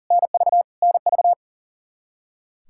Morse code, heard as a keyed tone, about 40 wpm, 700 Hz. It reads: NV NV
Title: K1USN SST QSO Practice 40wpm